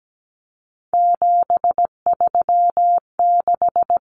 Morse code, heard as a keyed tone, about 17 wpm, 700 Hz. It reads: 736
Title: Common Cut Numbers - 3-Digit 17wpm